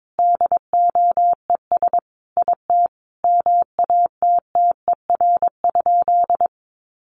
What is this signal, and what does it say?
Morse code, heard as a keyed tone, about 22 wpm, 700 Hz. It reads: DOES IT MATTER?
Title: Sentences from Top 700 Words 22wpm